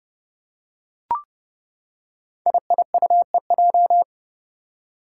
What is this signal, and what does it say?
Morse code, heard as a keyed tone, about 30 wpm, 700 Hz. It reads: IIUEJ
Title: Five Challenging Letter-Number Combinations 30wpm